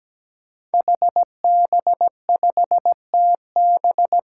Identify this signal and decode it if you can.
Morse code, heard as a keyed tone, about 17 wpm, 700 Hz. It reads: HB5TB
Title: Five Challenging Letter-Number Combinations 17wpm